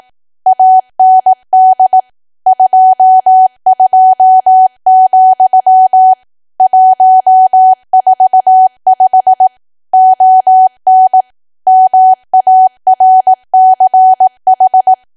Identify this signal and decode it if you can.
Morse code, heard as a keyed tone, about 18 wpm, 750 Hz. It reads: AND 22, 145 ON MARCH
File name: cwa / W1AW-Level2 / 18WPM